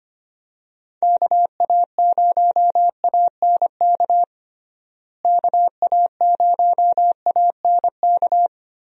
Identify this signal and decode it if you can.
Morse code, heard as a keyed tone, about 25 wpm, 700 Hz. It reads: KA0ANK KA0ANK
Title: USA Call Signs - 6 Characters - Repeated 3x - 25wpm